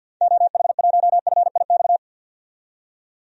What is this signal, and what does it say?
Morse code, heard as a keyed tone, about 52 wpm, 700 Hz. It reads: OH1FIX
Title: DXCC-100 Call Signs - 6 Characters - 50wpm